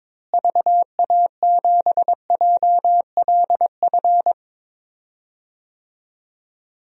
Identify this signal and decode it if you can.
Morse code, heard as a keyed tone, about 22 wpm, 700 Hz. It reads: VA7JLF
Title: DXCC-100 Call Signs - 6 Characters - Repeated 2x - 22wpm